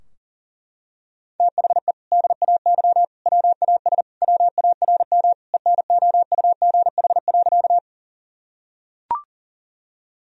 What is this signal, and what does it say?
Morse code, heard as a keyed tone, about 40 wpm, 700 Hz. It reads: THE DAY WAS WARM ENOUGH.
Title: Sentences from Top 1000 Words - Review Part 2 - 40wpm